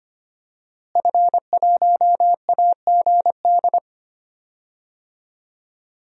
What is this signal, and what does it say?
Morse code, heard as a keyed tone, about 25 wpm, 700 Hz. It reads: F1AGD
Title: DXCC-100 Call Signs - 5 Characters - Repeated 3x - 25wpm